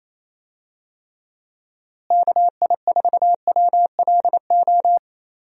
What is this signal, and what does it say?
Morse code, heard as a keyed tone, about 28 wpm, 700 Hz. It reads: KI4WLO
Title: USA Call Signs - 6 Characters - Repeated 2x - 28wpm